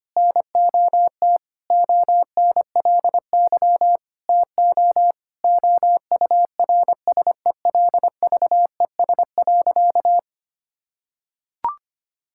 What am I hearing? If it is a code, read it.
Morse code, heard as a keyed tone, about 25 wpm, 700 Hz. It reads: NOT ONLY TO OURSELVES.
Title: Sentences from Top 1600 Words 25wpm